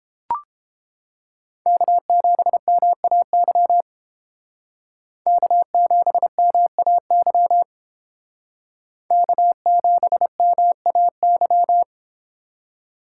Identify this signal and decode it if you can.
Morse code, heard as a keyed tone, about 29 wpm, 700 Hz. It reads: K7MAY K7MAY K7MAY